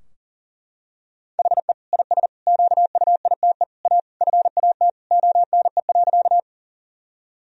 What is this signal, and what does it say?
Morse code, heard as a keyed tone, about 37 wpm, 700 Hz. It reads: HE IS QUITE A FAT ONE.
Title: Sentences from Top 1000 Words - Review Part 4 - 40wpm